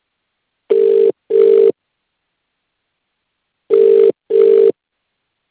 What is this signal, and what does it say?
Ringback tone (UK), 400 + 450 Hz.